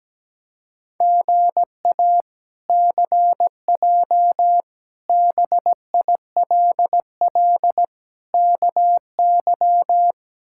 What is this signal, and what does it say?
Morse code, heard as a keyed tone, about 17 wpm, 700 Hz. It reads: GA CJ BILL KY
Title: K1USN SST QSO Practice - 17wpm